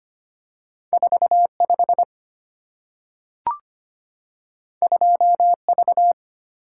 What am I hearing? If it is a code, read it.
Morse code, heard as a keyed tone, about 25 wpm, 700 Hz. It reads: 45 2V
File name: Two Letter-Number Combinations - 25wpm